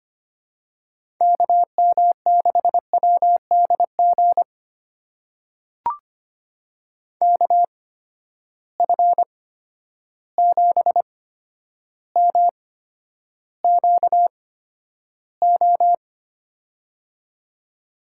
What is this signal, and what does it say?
Morse code, heard as a keyed tone, about 25 wpm, spaced as for 11 wpm, 700 Hz. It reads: KM6WDG KF7MQO
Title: USA Call Signs - 6 Characters - 8x Character Spacing - 25wpm